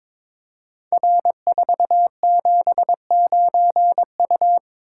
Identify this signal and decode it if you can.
Morse code, heard as a keyed tone, about 22 wpm, 700 Hz. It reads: R479U
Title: Code Groups - 5 Letter:Numbers - 22wpm